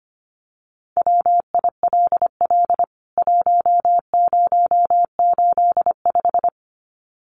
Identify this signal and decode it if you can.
Morse code, heard as a keyed tone, about 25 wpm, 700 Hz. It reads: WILL 1085
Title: CWT Exchange 25wpm